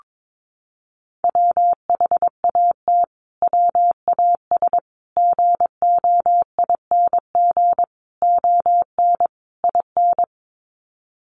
Morse code, heard as a keyed tone, about 22 wpm, 700 Hz. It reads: WHAT WAS GOING ON IN
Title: Sets of 5 Words Encore 22wpm